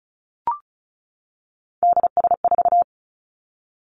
Morse code, heard as a keyed tone, about 36 wpm, 700 Hz. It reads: DS4
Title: Three Challenging Letter-Number Combinations 35wpm